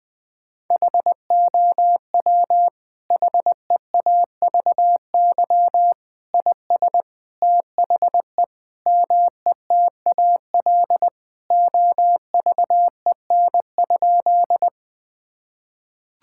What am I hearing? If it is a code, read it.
Morse code, heard as a keyed tone, about 20 wpm, 700 Hz. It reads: HOW HEAVY IS THE METAL OVEN?
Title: Sentences from Top 1000 Words - Review Part 6 - 20wpm